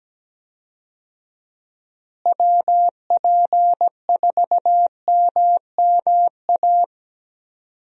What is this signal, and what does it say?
Morse code, heard as a keyed tone, about 17 wpm, 700 Hz. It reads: WP4MMA
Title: USA Call Signs - 6 Characters - Repeated 2x - 17wpm